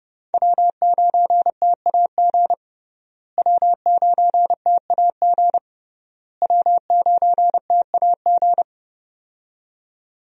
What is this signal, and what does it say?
Morse code, heard as a keyed tone, about 30 wpm, 700 Hz. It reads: W9TAG W9TAG W9TAG
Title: USA Call Signs - 5 Characters - Repeated 3x - 30wpm